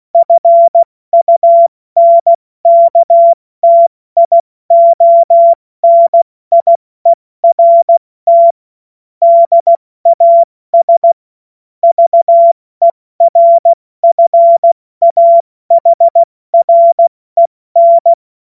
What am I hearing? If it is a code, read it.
Morse code, heard as a keyed tone, about 16 wpm, spaced as for 14 wpm, 680 Hz. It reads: FUNKTIONIERT DAS VERFAHREN
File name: Morsekurs / Witze / Witz 30. August 2022 70 F80